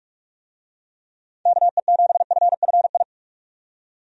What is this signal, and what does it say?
Morse code, heard as a keyed tone, about 44 wpm, 700 Hz. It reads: KE7FFI
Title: USA Call Signs - 6 Characters - 45wpm